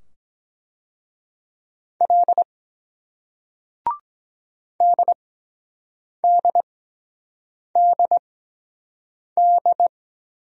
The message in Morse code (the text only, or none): L DDDD